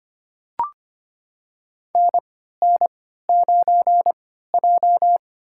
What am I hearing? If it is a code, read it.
Morse code, heard as a keyed tone, about 25 wpm, spaced as for 14 wpm, 700 Hz. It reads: NN9J
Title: USA Call Signs - 4 Characters - 3x Character Spacing - 25wpm